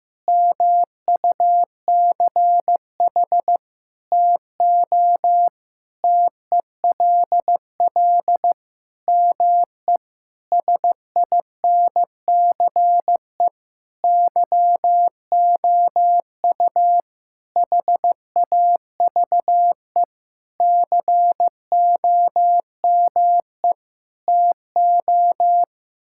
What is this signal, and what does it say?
Morse code, heard as a keyed tone, about 15 wpm, 700 Hz. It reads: MUCH TO TELL ME SINCE YOU HAVE COME TO
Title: Sentences from Top 1600 Words 15wpm